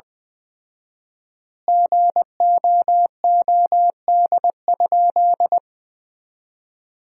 Morse code, heard as a keyed tone, about 21 wpm, 700 Hz. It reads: GOOD?
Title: 4x Word-Spacing - Sentences from Top 200 Words - 20wpm